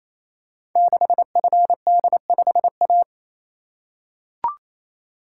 Morse code, heard as a keyed tone, about 28 wpm, 700 Hz. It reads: 6FD5A
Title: Code Groups - 5 Letter:Numbers - 28wpm